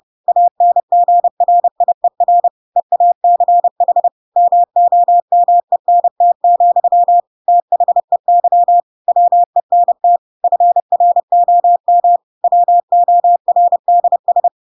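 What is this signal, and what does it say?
Morse code, heard as a keyed tone, about 30 wpm, 700 Hz. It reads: ANGRIER EACH MOMENT, THEY WENT FROM WORDS
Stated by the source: Chapter 2 of The Adventures of Pinocchio read as a book in Morse code (CW) at 30 wpm, by ik1hge.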